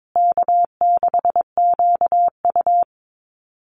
Morse code, heard as a keyed tone, about 22 wpm, 700 Hz. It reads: K6QU
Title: Easy Call Signs Repeated 2x - 22wpm